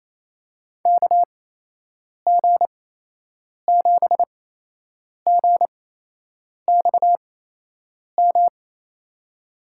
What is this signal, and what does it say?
Morse code, heard as a keyed tone, about 28 wpm, spaced as for 8 wpm, 700 Hz. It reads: KG7GXM
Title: USA Call Signs - 6 Characters - 8x Character Spacing - 28wpm